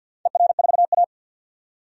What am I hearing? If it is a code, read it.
Morse code, heard as a keyed tone, about 48 wpm, 700 Hz. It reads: ER4A
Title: DXCC-100 Call Signs - 4 Characters - 50wpm